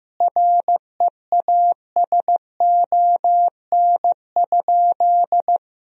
Morse code, heard as a keyed tone, about 15 wpm, 700 Hz. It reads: REASON?
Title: Sentences from Top 1000 Words - Review Part 1 - 15wpm